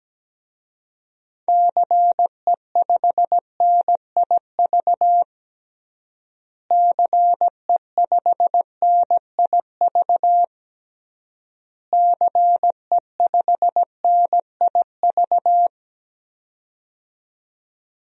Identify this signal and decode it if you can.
Morse code, heard as a keyed tone, about 17 wpm, 700 Hz. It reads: CE5NIV CE5NIV CE5NIV
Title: DXCC-100 Call Signs - 6 Characters - Repeated 3x - 17wpm